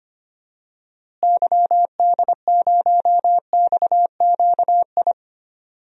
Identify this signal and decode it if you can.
Morse code, heard as a keyed tone, about 25 wpm, 700 Hz. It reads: YD0XQI
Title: DXCC-100 Call Signs - 6 Characters - Repeated 2x - 25wpm